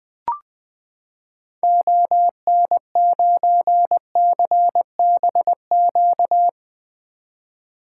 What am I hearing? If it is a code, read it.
Morse code, heard as a keyed tone, about 20 wpm, 700 Hz. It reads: ON9CBQ
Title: 2023 SOTA Activator Call Signs - 20wpm